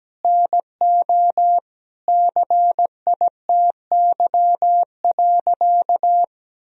Morse code, heard as a keyed tone, about 17 wpm, 700 Hz. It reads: NO CITY.
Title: Sentences from Top 1000 Words - Review Part 1 - 17wpm